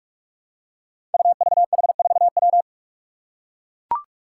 Morse code, heard as a keyed tone, about 44 wpm, 700 Hz. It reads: UVH4W
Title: Five Challenging Letter-Number Combinations 45wpm